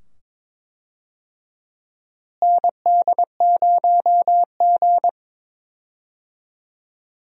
Morse code, heard as a keyed tone, about 22 wpm, 700 Hz. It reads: ND0G
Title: USA Call Signs - 4 Characters - Repeated 2x - 22wpm